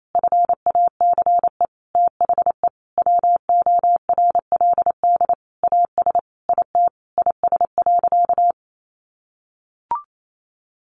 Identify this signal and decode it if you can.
Morse code, heard as a keyed tone, about 28 wpm, 700 Hz. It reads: FACE THE WORLD AS IT IS.
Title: Sentences from Top 1200 Words 28wpm